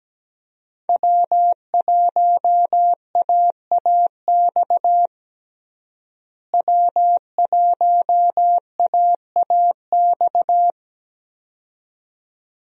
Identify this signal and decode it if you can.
Morse code, heard as a keyed tone, about 17 wpm, 700 Hz. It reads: W1AAX W1AAX
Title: USA Call Signs - 5 Characters - Repeated 2x - 17wpm